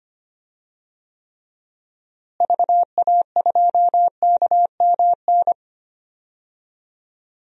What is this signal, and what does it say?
Morse code, heard as a keyed tone, about 25 wpm, 700 Hz. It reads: VA2KMN
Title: DXCC-100 Call Signs - 6 Characters - Repeated 3x - 25wpm